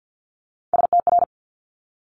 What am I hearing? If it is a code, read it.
Morse code, heard as a keyed tone, about 50 wpm, 700 Hz. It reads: STR
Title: Multi-letter Phonemes Quick Fire 50wpm